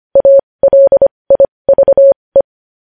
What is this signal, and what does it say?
Morse code, heard as a keyed tone, about 25 wpm, spaced as for 20 wpm, 550 Hz. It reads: ALIVE
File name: podcast / News headlines-1x-25x20